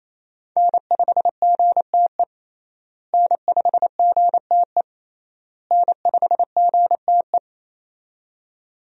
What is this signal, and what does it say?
Morse code, heard as a keyed tone, about 27 wpm, 700 Hz. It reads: N5GTE N5GTE N5GTE
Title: USA Call Signs - 5 Characters - Repeated 3x - 28wpm